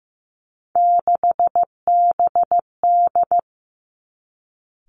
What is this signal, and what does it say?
Morse code, heard as a keyed tone, about 15 wpm, 700 Hz. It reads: 6BD
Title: Three Challenging Letter-Number Combinations 15wpm